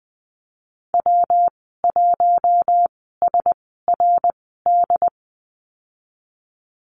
Morse code, heard as a keyed tone, about 20 wpm, spaced as for 15 wpm, 700 Hz. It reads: W1SRD
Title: 128 Lesson 21g - Number 6 - Call Signs 20wpm